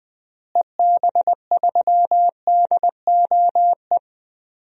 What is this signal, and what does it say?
Morse code, heard as a keyed tone, about 21 wpm, 700 Hz. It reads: EB3DOE
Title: DXCC-100 Call Signs - 6 Characters - Repeated 3x - 20wpm